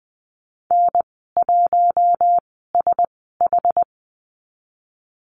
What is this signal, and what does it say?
Morse code, heard as a keyed tone, about 20 wpm, spaced as for 15 wpm, 700 Hz. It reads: N1SH
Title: 082 Lesson 14g - Number 2 - Call Signs 20wpm